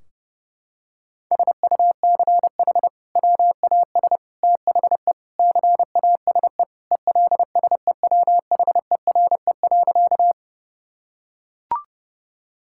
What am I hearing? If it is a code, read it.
Morse code, heard as a keyed tone, about 30 wpm, 700 Hz. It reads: SUCH WAS THE CASE ELSEWHERE.
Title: Sentences from Top 1600 Words 30wpm